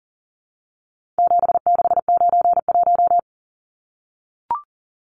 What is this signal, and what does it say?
Morse code, heard as a keyed tone, about 40 wpm, 700 Hz. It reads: 7691
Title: Numbers - 4-Digit 40wpm